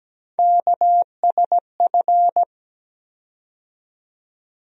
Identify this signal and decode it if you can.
Morse code, heard as a keyed tone, about 17 wpm, 700 Hz. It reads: KSF
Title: USA Call Sign Postfixes - 3 Letters 17wpm